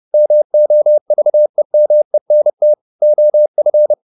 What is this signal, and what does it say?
Morse code, heard as a keyed tone, about 30 wpm, 600 Hz. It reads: MOVEMENT OF